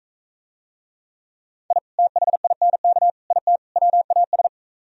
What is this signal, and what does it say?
Morse code, heard as a keyed tone, about 45 wpm, 700 Hz. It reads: I THINK IT WAS